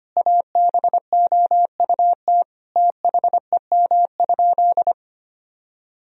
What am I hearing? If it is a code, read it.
Morse code, heard as a keyed tone, about 25 wpm, 700 Hz. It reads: ABOUT THEM?
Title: Sentences from Top 1000 Words - Review Part 3 - 25wpm